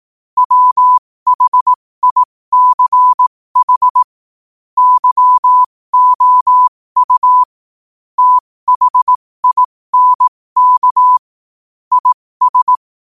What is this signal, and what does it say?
Morse code, heard as a keyed tone, about 18 wpm, spaced as for 15 wpm, 1000 Hz. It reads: WHICH YOU THINK IS
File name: qotdfiles / 15wpm